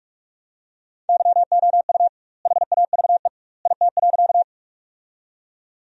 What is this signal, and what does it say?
Morse code, heard as a keyed tone, about 46 wpm, 700 Hz. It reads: YOU HAVE IT.
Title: 2x Word-Spacing - Sentences from Top 100 Words - 45wpm